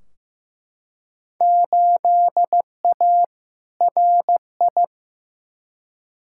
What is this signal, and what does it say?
Morse code, heard as a keyed tone, about 15 wpm, 700 Hz. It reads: 8A RI
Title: Field Day Exchanges Encore - 15wpm